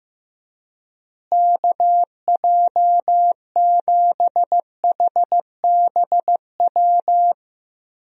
Morse code, heard as a keyed tone, about 15 wpm, 700 Hz. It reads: KJ7HBW
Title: USA Call Signs - 6 Characters - Repeated 3x - 15wpm